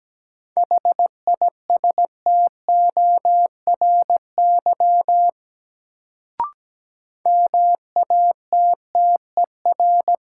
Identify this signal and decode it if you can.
Morse code, heard as a keyed tone, about 17 wpm, 700 Hz. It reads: HISTORY MATTER
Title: Sets of 3 Words - New Word Drill - 17wpm